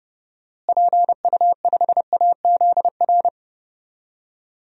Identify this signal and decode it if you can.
Morse code, heard as a keyed tone, about 32 wpm, 700 Hz. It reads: PU5AZR
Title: DXCC-100 Call Signs - 6 Characters - Repeated 3x - 30wpm